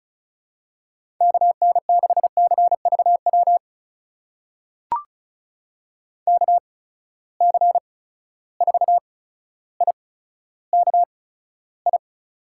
Morse code, heard as a keyed tone, about 34 wpm, spaced as for 15 wpm, 700 Hz. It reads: KN6CVW KC4IKI
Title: USA Call Signs - 6 Characters - 8x Character Spacing - 35wpm